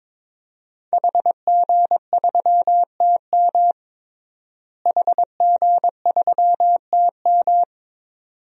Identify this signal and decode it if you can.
Morse code, heard as a keyed tone, about 22 wpm, 700 Hz. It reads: HG3TM HG3TM
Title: DXCC-100 Call Signs - 5 Characters - Repeated 2x - 22wpm